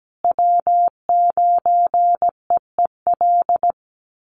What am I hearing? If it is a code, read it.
Morse code, heard as a keyed tone, about 17 wpm, 700 Hz. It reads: W9EEL
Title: USA Call Signs Repeated 2x 17wpm